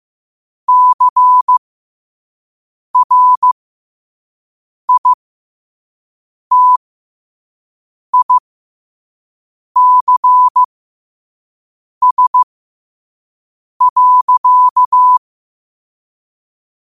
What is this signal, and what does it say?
Morse code, heard as a keyed tone, about 15 wpm, spaced as for 5 wpm, 1000 Hz. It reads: CRITICS.
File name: qotdfiles / 5wpm